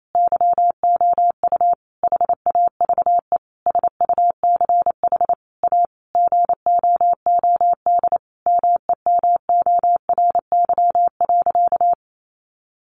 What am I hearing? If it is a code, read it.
Morse code, heard as a keyed tone, about 28 wpm, 700 Hz. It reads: YOU HAVE SUCH A GOOD MEMORY.
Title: Sentences from Top 1300 Words 28wpm